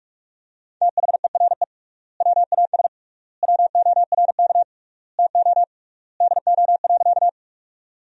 Morse code, heard as a keyed tone, about 46 wpm, 700 Hz. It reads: THERE WAS WORK TO DO.
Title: 3x Word-Spacing - Sentences from Top 200 Words - 45wpm